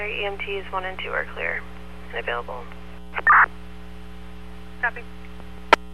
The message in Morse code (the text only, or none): none